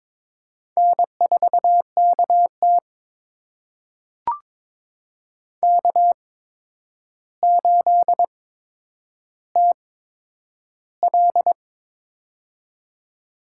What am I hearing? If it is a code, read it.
Morse code, heard as a keyed tone, about 22 wpm, spaced as for 9 wpm, 700 Hz. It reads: N4KT K8TL
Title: USA Call Signs - 4 Characters - 8x Character Spacing - 22wpm